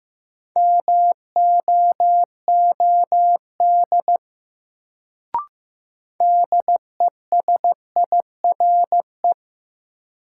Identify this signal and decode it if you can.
Morse code, heard as a keyed tone, about 15 wpm, 700 Hz. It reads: MOOD DESIRE
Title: Sentences from Top 1100 Words - New Word Drill - 15wpm